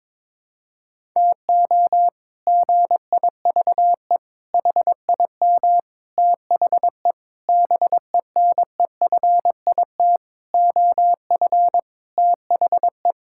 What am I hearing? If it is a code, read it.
Morse code, heard as a keyed tone, about 22 wpm, 700 Hz. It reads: TO GIVE HIM THE BENEFIT OF THE
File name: practice / 22wpm / Sets of 8 Words - 22wpm